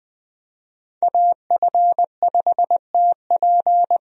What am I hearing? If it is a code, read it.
Morse code, heard as a keyed tone, about 20 wpm, 700 Hz. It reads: AF5TP
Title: USA Call Signs - 5 Characters - 20wpm